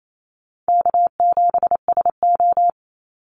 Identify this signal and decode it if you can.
Morse code, heard as a keyed tone, about 28 wpm, 700 Hz. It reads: K7SO
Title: NA SOTA Activator Callsigns 28wpm